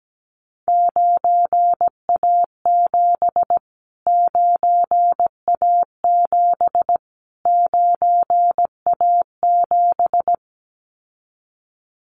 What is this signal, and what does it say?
Morse code, heard as a keyed tone, about 17 wpm, 700 Hz. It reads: 9A7 9A7 9A7
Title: Callsign Prefixes for Top 25 DXCC - Repeated 3x - 17wpm